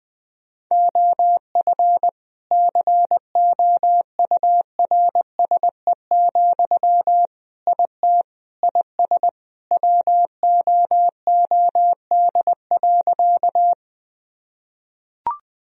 Morse code, heard as a keyed tone, about 21 wpm, 700 Hz. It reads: OF COURSE, IT IS WOOD.
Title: Sentences from Top 1000 Words - Review Part 2 - 20wpm